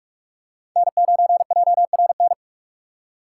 Morse code, heard as a keyed tone, about 44 wpm, 700 Hz. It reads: N9JRN
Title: USA Call Signs - 5 Characters - 2x Character Spacing - 45wpm